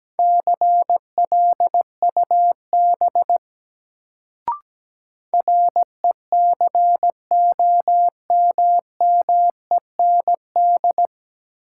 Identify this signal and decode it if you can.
Morse code, heard as a keyed tone, about 17 wpm, 700 Hz. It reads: CLUB RECOMMEND
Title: Sentences from Top 1200 Words - New Word Drill Alt - 17wpm